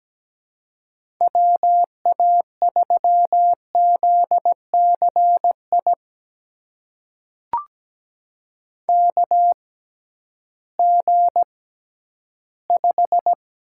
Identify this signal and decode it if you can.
Morse code, heard as a keyed tone, about 17 wpm, spaced as for 11 wpm, 700 Hz. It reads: WA3ZCI KG5
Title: USA Call Signs - 6 Characters - 6x Character Spacing - 17wpm